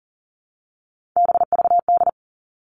Morse code, heard as a keyed tone, about 40 wpm, 700 Hz. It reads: BVD
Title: Three Challenging Letter-Number Combinations 40wpm